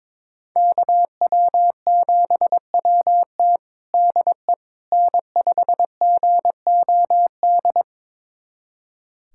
Morse code, heard as a keyed tone, about 22 wpm, 700 Hz. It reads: KW7WT DE N5GOD
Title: USA Callsign DE Callsign - 5 Characters - 22wpm